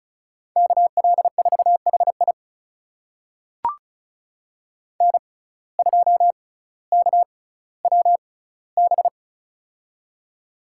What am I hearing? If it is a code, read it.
Morse code, heard as a keyed tone, about 36 wpm, spaced as for 18 wpm, 700 Hz. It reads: KL4HI N2KWB
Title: USA Call Signs - 6x Character Spacing - 35wpm